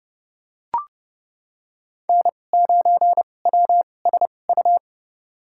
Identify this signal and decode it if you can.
Morse code, heard as a keyed tone, about 30 wpm, spaced as for 22 wpm, 700 Hz. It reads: N9WSU
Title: USA Call Signs - 5 Characters - 2x Character Spacing - 30wpm